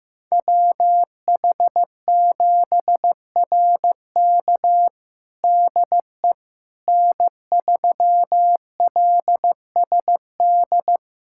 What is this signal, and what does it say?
Morse code, heard as a keyed tone, about 15 wpm, 700 Hz. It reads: WH7RK DE N3LSD
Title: USA Callsign DE Callsign - 5 Characters - 15wpm